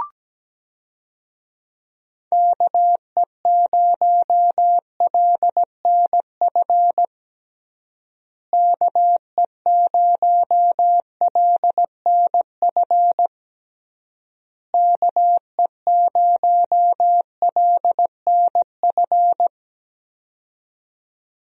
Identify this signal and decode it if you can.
Morse code, heard as a keyed tone, about 17 wpm, 700 Hz. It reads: KE0LNF KE0LNF KE0LNF
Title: USA Call Signs - 6 Characters - Repeated 3x - 17wpm